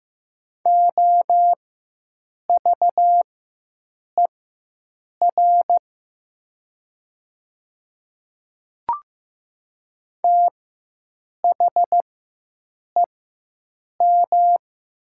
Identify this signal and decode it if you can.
Morse code, heard as a keyed tone, about 15 wpm, spaced as for 7 wpm, 700 Hz. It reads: OVER THEM
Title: 4x Character Spacing - Top 100 Words - 15wpm